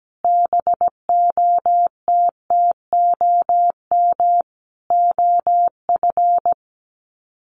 Morse code, heard as a keyed tone, about 17 wpm, 700 Hz. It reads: BOTTOM OF
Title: Sets of 5 Words Encore 17wpm